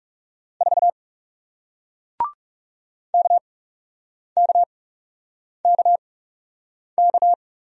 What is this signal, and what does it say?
Morse code, heard as a keyed tone, about 41 wpm, spaced as for 8 wpm, 700 Hz. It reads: 4 KKKK